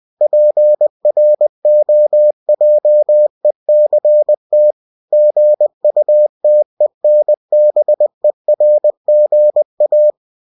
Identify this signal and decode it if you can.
Morse code, heard as a keyed tone, about 20 wpm, 600 Hz. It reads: PROJECT GUTENBERGA
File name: cw-books / winnie-the-pooh / winnie-20wpm-page-0116